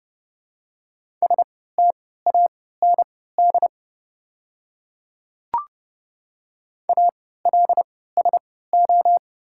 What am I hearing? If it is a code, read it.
Morse code, heard as a keyed tone, about 30 wpm, spaced as for 17 wpm, 700 Hz. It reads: STAND ALSO
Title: 3x Character Spacing - Top 300 Words - 30wpm